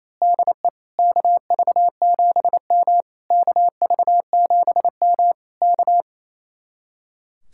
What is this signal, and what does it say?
Morse code, heard as a keyed tone, about 26 wpm, 700 Hz. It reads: DE KV7M KV7M K
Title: Speed-Racing Common QSO Phrases - 28wpm